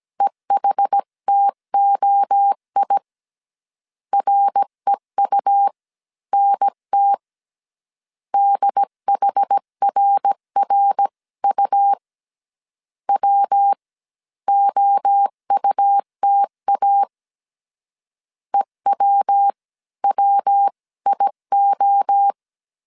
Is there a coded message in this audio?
Yes: Morse code from 0.2 to 22.3 s, heard as a keyed tone, about 17 wpm, spaced as for 14 wpm, 800 Hz. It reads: EHTOI REUNT DHRRU WOUTA EWWIO